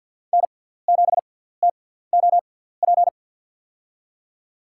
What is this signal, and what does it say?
Morse code, heard as a keyed tone, about 53 wpm, spaced as for 17 wpm, 700 Hz. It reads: N7TOP
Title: USA Call Signs - 5 Characters - 6x Character Spacing - 50wpm